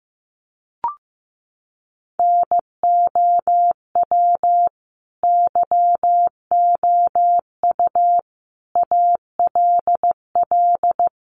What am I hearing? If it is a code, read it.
Morse code, heard as a keyed tone, about 15 wpm, 700 Hz. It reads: NOW YOU ALL
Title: Sentences from Top 1200 Words 15wpm